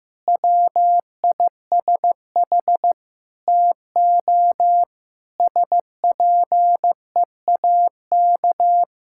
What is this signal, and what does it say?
Morse code, heard as a keyed tone, about 15 wpm, 700 Hz. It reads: WISH TO SPEAK
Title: Sentences from Top 1600 Words 15wpm